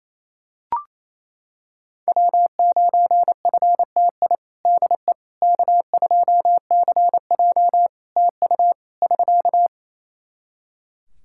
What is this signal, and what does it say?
Morse code, heard as a keyed tone, about 28 wpm, 700 Hz. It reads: W9FTI DE K2CJ TU <SK>
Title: K1USN SST QSO Practice 28wpm